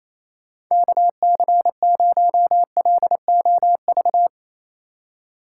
Morse code, heard as a keyed tone, about 29 wpm, 700 Hz. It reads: KC0LOV
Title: USA Call Signs - 6 Characters - 4x Character Spacing - 28wpm